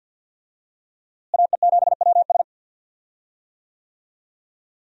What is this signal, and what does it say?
Morse code, heard as a keyed tone, about 46 wpm, 700 Hz. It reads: AE7WS
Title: USA Call Signs - 5 Characters - Repeated 2x - 50wpm